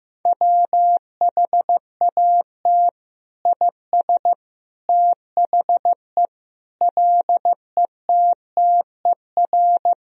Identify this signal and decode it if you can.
Morse code, heard as a keyed tone, about 15 wpm, 700 Hz. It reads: WHAT IS THE LETTER
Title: Sentences from Top 1000 Words - Review Part 2 - 15wpm